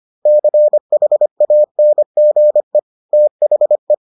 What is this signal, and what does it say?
Morse code, heard as a keyed tone, about 25 wpm, 600 Hz. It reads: CHANGE THE